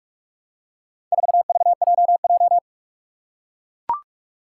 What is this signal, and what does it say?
Morse code, heard as a keyed tone, about 44 wpm, 700 Hz. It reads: 4VJJ
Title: Four Challenging Letter-Number Combinations 45wpm